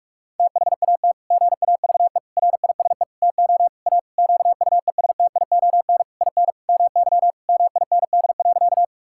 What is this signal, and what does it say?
Morse code, heard as a keyed tone, about 41 wpm, 700 Hz. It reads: THAT GAVE RISE TO A QUESTION IN MY MIND.
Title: Sentences from Top 1000 Words - Review Part 6 - 45wpm